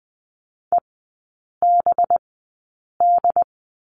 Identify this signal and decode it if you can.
Morse code, heard as a keyed tone, about 20 wpm, spaced as for 8 wpm, 690 Hz. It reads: EBD
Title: 212 Lesson 34e - Number 8 - Three Characters 20wpm